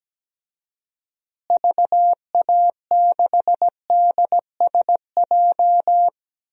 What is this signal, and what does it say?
Morse code, heard as a keyed tone, about 17 wpm, 700 Hz. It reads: VA6DSJ
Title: DXCC-100 Call Signs - 6 Characters - Repeated 2x - 17wpm